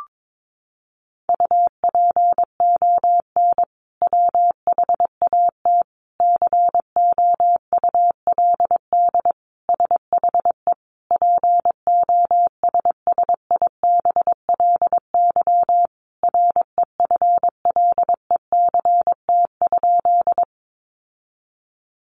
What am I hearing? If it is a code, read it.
Morse code, heard as a keyed tone, about 22 wpm, 700 Hz. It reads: UPON WHAT COULD SHE POSSIBLY REFLECT?
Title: Sentences from Top 1500 Words 22wpm